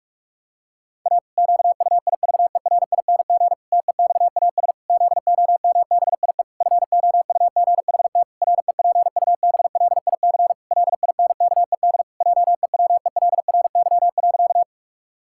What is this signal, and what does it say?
Morse code, heard as a keyed tone, about 46 wpm, 700 Hz. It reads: A QUIVERING TEXAS ZOMBIE FOUGHT REPUBLIC LINKED JEWELRY.
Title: Pangram Challenge v2 - 45wpm